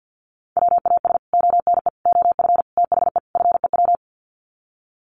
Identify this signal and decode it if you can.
Morse code, heard as a keyed tone, about 50 wpm, 700 Hz. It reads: WAS ONE OF THE FEW
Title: Sets of 5 Words Encore 50wpm